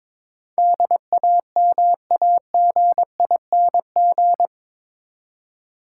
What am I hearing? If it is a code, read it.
Morse code, heard as a keyed tone, about 22 wpm, 700 Hz. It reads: DAMAGING
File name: practice / 22wpm / Eight Letter Words - 22wpm